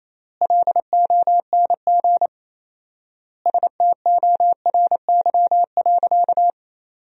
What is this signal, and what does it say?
Morse code, heard as a keyed tone, about 28 wpm, 700 Hz. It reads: LONG STORY.
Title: 4x Word-Spacing - Sentences from Top 300 Words - 28wpm